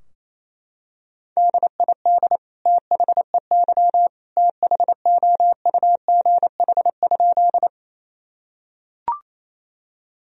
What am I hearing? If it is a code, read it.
Morse code, heard as a keyed tone, about 28 wpm, 700 Hz. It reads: DID THEY THOUGH?
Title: Sentences from Top 1000 Words - Review Part 2 - 28wpm